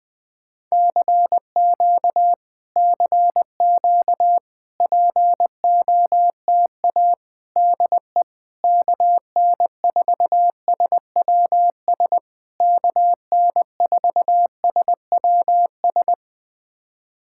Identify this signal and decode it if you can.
Morse code, heard as a keyed tone, about 20 wpm, 700 Hz. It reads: CQ CQ POTA DE KN4SWS KN4SWS
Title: POTA QSO Practice - 20wpm